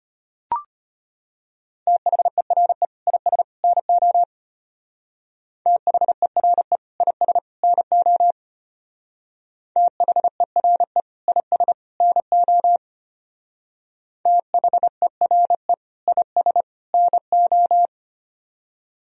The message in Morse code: THERE IS NO THERE IS NO THERE IS NO THERE IS NO

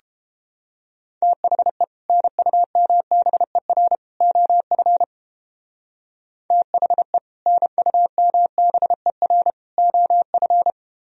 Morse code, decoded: THE NUMBER OF THE NUMBER OF

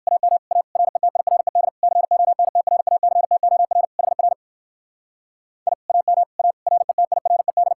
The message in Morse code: AM A RETIRED CONTRACTOR HR I AM A RETIRED

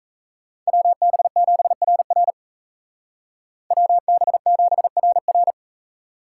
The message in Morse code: WB7RR WB7RR

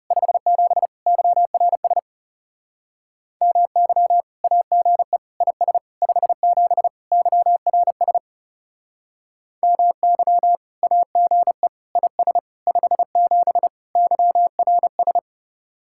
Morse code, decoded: 57 YRS MY AGE IS 57 YRS MY AGE IS 57 YRS

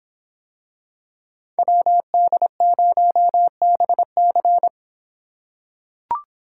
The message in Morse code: WD0BC

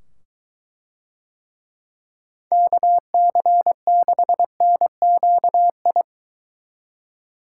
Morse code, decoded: KC6NQI

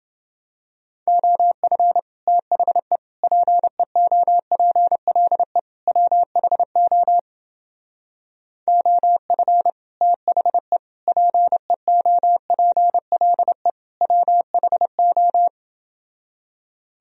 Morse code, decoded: OF THE PEOPLE WHO OF THE PEOPLE WHO

